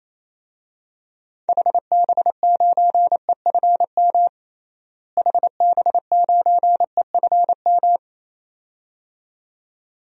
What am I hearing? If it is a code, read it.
Morse code, heard as a keyed tone, about 28 wpm, 700 Hz. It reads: HB9EFM HB9EFM
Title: DXCC-100 Call Signs - 6 Characters - Repeated 2x - 28wpm